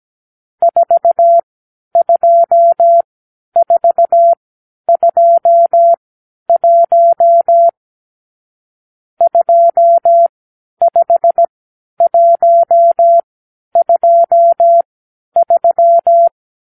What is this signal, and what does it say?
Morse code, heard as a keyed tone, about 17 wpm, spaced as for 10 wpm, 680 Hz. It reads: 42421 25123
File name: k7qo / Lesson 50 (1-6)